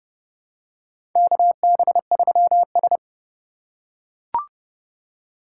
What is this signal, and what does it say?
Morse code, heard as a keyed tone, about 30 wpm, 700 Hz. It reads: KB3S
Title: USA Call Signs - 4 Characters - 8x Character Spacing - 30wpm